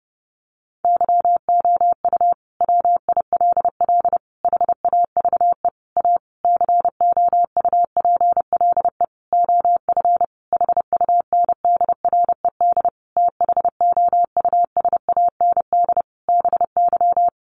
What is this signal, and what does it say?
Morse code, heard as a keyed tone, about 30 wpm, 700 Hz. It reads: YOU WILL HAVE A COUPLE OF HUNDRED THOUSAND BY
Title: Sentences from Top 1100 Words 30wpm